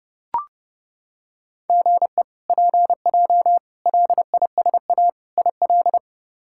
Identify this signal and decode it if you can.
Morse code, heard as a keyed tone, about 32 wpm, 700 Hz. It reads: GE PJ LISA IL
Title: K1USN SST QSO Practice - 30wpm